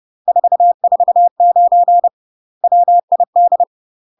Morse code, heard as a keyed tone, about 30 wpm, 700 Hz. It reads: 449 WID
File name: UK MASTER QSOs TRAINING / 30WPM / UK MASTER QSOs 0042